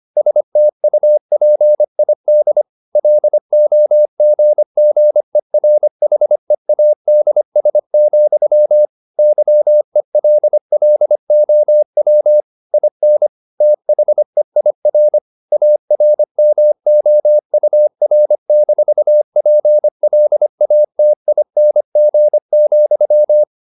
Morse code, heard as a keyed tone, about 25 wpm, 600 Hz. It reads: STUPID LOGGERHEADS, YELLOW IN THEIR ARMOUR-PLATING,
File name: the old man and the sea part 1 25wpm